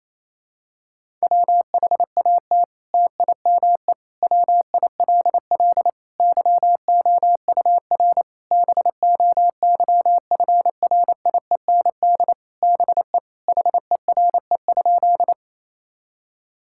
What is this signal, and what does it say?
Morse code, heard as a keyed tone, about 28 wpm, 700 Hz. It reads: WHAT TIME WILL YOUR BOYFRIEND BE HERE?
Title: Sentences from Top 1000 Words - Review Part 5 - 28wpm